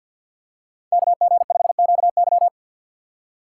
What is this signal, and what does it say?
Morse code, heard as a keyed tone, about 53 wpm, 700 Hz. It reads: KG5QY
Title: USA Call Signs - 5 Characters - 6x Character Spacing - 50wpm